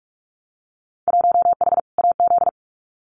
Morse code, heard as a keyed tone, about 46 wpm, 700 Hz. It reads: 1H AZ
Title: Winter Field Day Exchanges 45wpm